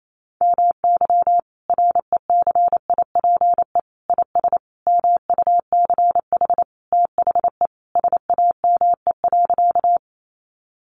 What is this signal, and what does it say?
Morse code, heard as a keyed tone, about 28 wpm, 700 Hz. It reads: MY RECIPE IS MUCH THE SAME.
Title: Sentences from Top 1200 Words 28wpm